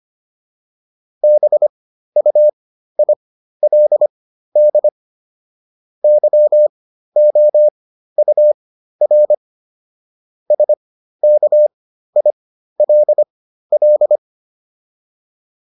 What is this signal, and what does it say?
Morse code, heard as a keyed tone, about 25 wpm, spaced as for 13 wpm, 600 Hz. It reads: BUILD YOUR SKILL